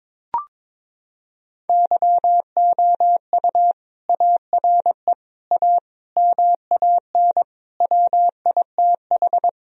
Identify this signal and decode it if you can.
Morse code, heard as a keyed tone, about 22 wpm, 700 Hz. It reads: YOU ARE A MAN WITH
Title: Sentences from Top 1000 Words - Review Part 3 - 22wpm